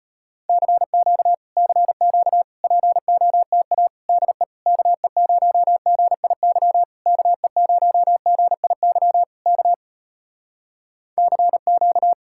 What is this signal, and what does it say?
Morse code, heard as a keyed tone, about 35 wpm, 700 Hz. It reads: CQ CQ POTA DE KE0GIY KE0GIY K CQ